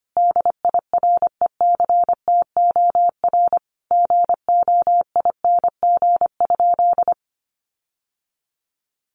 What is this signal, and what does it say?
Morse code, heard as a keyed tone, about 25 wpm, 700 Hz. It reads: DIRECTOR GOING?
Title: Sentences from Top 1300 Words 25wpm